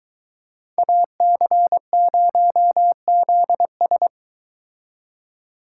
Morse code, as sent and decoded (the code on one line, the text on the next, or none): .- -.-. ----- --.. ...
AC0ZS